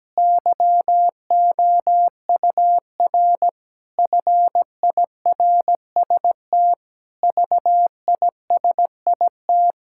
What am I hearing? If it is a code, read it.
Morse code, heard as a keyed tone, about 17 wpm, 700 Hz. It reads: YOUR FIRST VISIT
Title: Sentences from Top 1000 Words - Review Part 6 - 17wpm